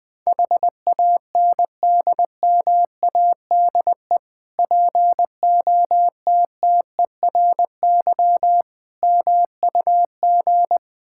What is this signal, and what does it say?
Morse code, heard as a keyed tone, about 21 wpm, 690 Hz. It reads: HANDMADE POTTERY MUG
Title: Difficult Words with Hints - 20wpm